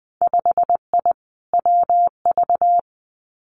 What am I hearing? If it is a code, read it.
Morse code, heard as a keyed tone, about 20 wpm, 700 Hz. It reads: 5I WV
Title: Winter Field Day Exchanges Encore 20wpm